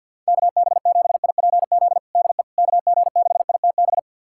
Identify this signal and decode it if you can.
Morse code, heard as a keyed tone, about 50 wpm, 700 Hz. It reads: KB7IPZ DE KG6ITB
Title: USA Callsign DE Callsign 50wpm